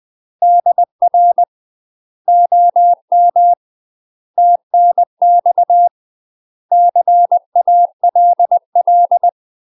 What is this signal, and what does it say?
Morse code, heard as a keyed tone, about 20 wpm, 700 Hz. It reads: DR OM TNX CALL
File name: UK MASTER QSOs TRAINING / 20WPM / UK MASTER QSOs 0029